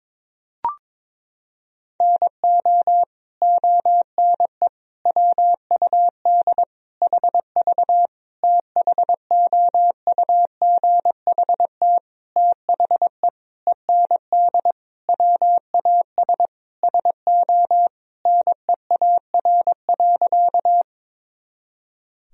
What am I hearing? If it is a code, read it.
Morse code, heard as a keyed tone, about 22 wpm, 700 Hz. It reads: NO ONE WUD HV THOUGHT THE END WAS SO NEAR.
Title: Sentences from Top 300 Words and Abbreviations 22wpm